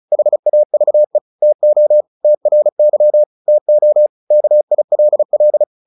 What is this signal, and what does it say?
Morse code, heard as a keyed tone, about 34 wpm, 600 Hz. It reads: HAVE TO TRY TO KILL
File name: the old man and the sea 2 35wpm